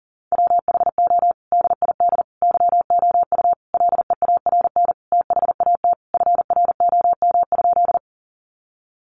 Morse code, heard as a keyed tone, about 37 wpm, 700 Hz. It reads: WHO DID YOU LEARN THAT FROM?
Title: Sentences from Top 300 Words 40wpm